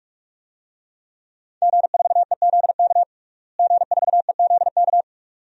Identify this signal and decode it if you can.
Morse code, heard as a keyed tone, about 41 wpm, 700 Hz. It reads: G4EZK G4EZK
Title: DXCC-100 Call Signs - 5 Characters - Repeated 2x - 45wpm